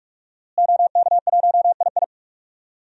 Morse code, heard as a keyed tone, about 46 wpm, 700 Hz. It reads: OK1II
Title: DXCC-100 Call Signs - 5 Characters - Repeated 3x - 45wpm